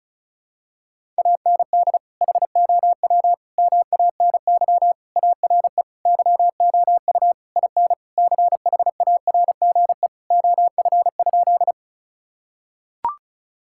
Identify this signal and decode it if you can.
Morse code, heard as a keyed tone, about 34 wpm, 700 Hz. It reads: AND HOW MANY ARE YOU IN CHARGE OF?
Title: Sentences from Top 1000 Words - Review Part 6 - 35wpm